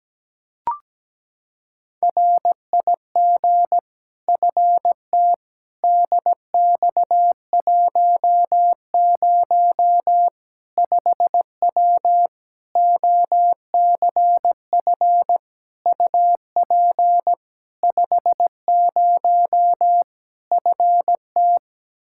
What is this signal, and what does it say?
Morse code, heard as a keyed tone, about 17 wpm, 700 Hz. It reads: RIG FT DX10 5W OCF UP 50 FT
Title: Practice QSO - Exchange 2 - Info - 17wpm